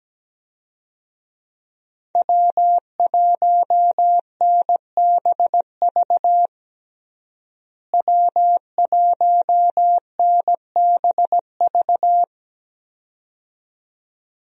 Morse code, heard as a keyed tone, about 17 wpm, 700 Hz. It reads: W1NBV W1NBV
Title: USA Call Signs - 5 Characters - Repeated 2x - 17wpm